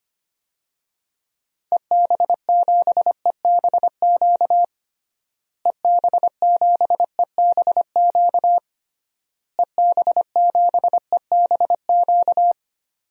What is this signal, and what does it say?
Morse code, heard as a keyed tone, about 25 wpm, 700 Hz. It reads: EB7EBQ EB7EBQ EB7EBQ
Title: DXCC-100 Call Signs - 6 Characters - Repeated 3x - 25wpm